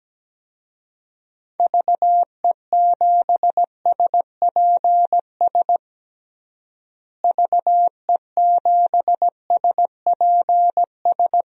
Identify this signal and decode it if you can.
Morse code, heard as a keyed tone, about 17 wpm, 700 Hz. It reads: VE7SPS VE7SPS
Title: DXCC-100 Call Signs - 6 Characters - Repeated 2x - 17wpm